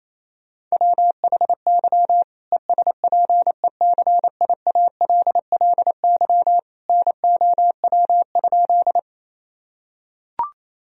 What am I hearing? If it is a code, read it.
Morse code, heard as a keyed tone, about 28 wpm, 700 Hz. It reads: WHY ESPECIALLY NOW?
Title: Sentences from Top 1000 Words - Review Part 4 - 28wpm